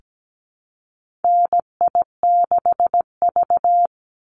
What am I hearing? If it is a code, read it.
Morse code, heard as a keyed tone, about 17 wpm, 700 Hz. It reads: NI6V
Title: Call Signs - 17wpm